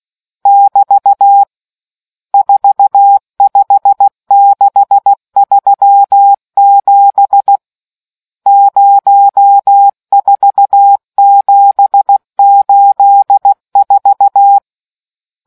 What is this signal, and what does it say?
Morse code, heard as a keyed tone, about 16 wpm, 800 Hz. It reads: <BT> 45637 04784